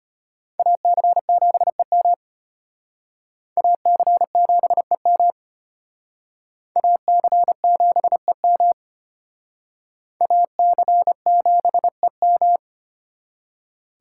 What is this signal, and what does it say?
Morse code, heard as a keyed tone, about 34 wpm, 700 Hz. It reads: AC7EM AC7EM AC7EM AC7EM